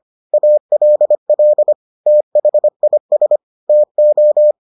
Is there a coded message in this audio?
Yes: Morse code from 0.3 to 4.5 s, heard as a keyed tone, about 25 wpm, 600 Hz. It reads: ALL THIS TO